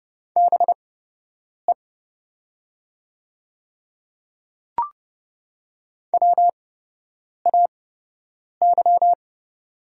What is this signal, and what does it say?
Morse code, heard as a keyed tone, about 30 wpm, spaced as for 8 wpm, 700 Hz. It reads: BE WAY
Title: 8x Character Spacing - Top 200 Words - 30wpm